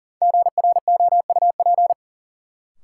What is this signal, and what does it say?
Morse code, heard as a keyed tone, about 40 wpm, 700 Hz. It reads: GROUP
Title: Sentences from Top 1000 Words - Review Part 1 - New Word Drill - 40wpm